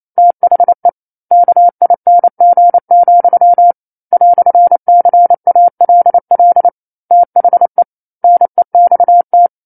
Morse code, heard as a keyed tone, about 28 wpm, 700 Hz. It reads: THE KING, "CALL THE NEXT
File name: 09-Chapter-09 28 WPM